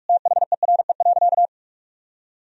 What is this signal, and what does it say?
Morse code, heard as a keyed tone, about 46 wpm, 700 Hz. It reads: THERE.
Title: Sentences from Top 300 Words and Abbreviations 45wpm